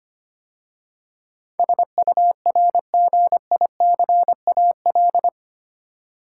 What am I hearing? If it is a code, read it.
Morse code, heard as a keyed tone, about 25 wpm, 690 Hz. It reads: SURGICAL
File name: 8 Letter Words - First 3 Letters Given - 25wpm